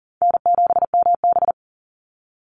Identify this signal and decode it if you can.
Morse code, heard as a keyed tone, about 40 wpm, 700 Hz. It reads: N7MB
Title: Call Signs - 40wpm